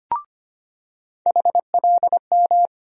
Morse code, heard as a keyed tone, about 25 wpm, 700 Hz. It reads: HLM
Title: USA Call Sign Postfixes - 3 Letters 25wpm